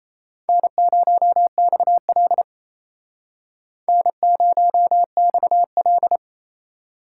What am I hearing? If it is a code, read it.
Morse code, heard as a keyed tone, about 35 wpm, 700 Hz. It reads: N0XL N0XL